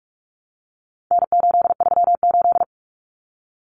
Morse code, heard as a keyed tone, about 44 wpm, 700 Hz. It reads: N838
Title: Common Cut Numbers - 4-Digit 45wpm